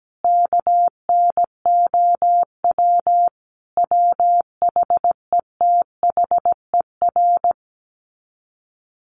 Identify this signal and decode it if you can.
Morse code, heard as a keyed tone, about 17 wpm, 700 Hz. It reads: KNOW WHETHER
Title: Sets of 5 Words 17wpm